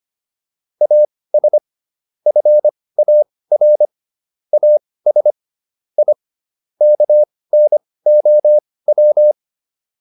Morse code, heard as a keyed tone, about 25 wpm, spaced as for 18 wpm, 600 Hz. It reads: AS FAR AS I KNOW